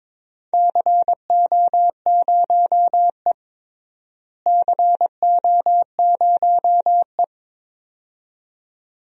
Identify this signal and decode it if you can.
Morse code, heard as a keyed tone, about 22 wpm, 700 Hz. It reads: CO0E CO0E
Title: DXCC-100 Call Signs - 4 Characters - Repeated 2x - 22wpm